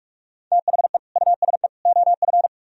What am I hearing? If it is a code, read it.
Morse code, heard as a keyed tone, about 44 wpm, 700 Hz. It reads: THE USE OF